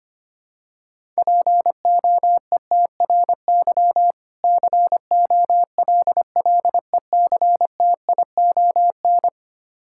Morse code, heard as a keyed tone, about 25 wpm, 700 Hz. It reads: POETRY COLLECTION
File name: practice / 25wpm / Difficult Words with Hints - 25wpm